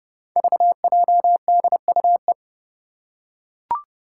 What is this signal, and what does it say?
Morse code, heard as a keyed tone, about 31 wpm, 700 Hz. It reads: VJDUE
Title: Five Challenging Letter-Number Combinations 30wpm